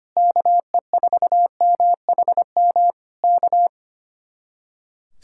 Morse code, heard as a keyed tone, about 25 wpm, 700 Hz. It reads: KE4MHM K
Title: Speed-Racing Common QSO Phrases - 25wpm